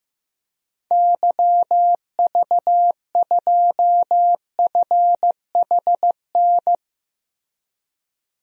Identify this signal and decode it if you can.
Morse code, heard as a keyed tone, about 15 wpm, 700 Hz. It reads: YV2FHN
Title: DXCC-100 Call Signs - 6 Characters - Repeated 3x - 15wpm